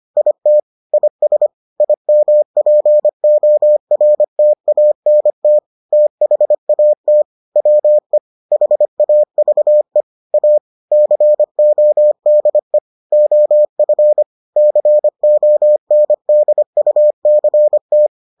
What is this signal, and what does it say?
Morse code, heard as a keyed tone, about 25 wpm, 600 Hz. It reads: IT IS IMPORTANT THAT WE HAVE A CODE OF CONDUCT